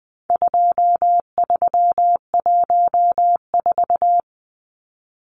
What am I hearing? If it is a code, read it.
Morse code, heard as a keyed tone, about 20 wpm, 700 Hz. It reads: 2314
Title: Numbers - 4-Digit 20wpm